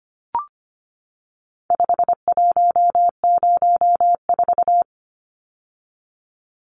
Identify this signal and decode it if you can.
Morse code, heard as a keyed tone, about 25 wpm, 700 Hz. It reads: 5104
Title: Numbers - 4-Digit 25wpm